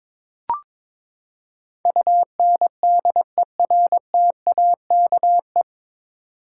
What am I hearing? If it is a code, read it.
Morse code, heard as a keyed tone, about 22 wpm, 700 Hz. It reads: UNDERTAKE
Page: Morse Code Ninja practice 22wpm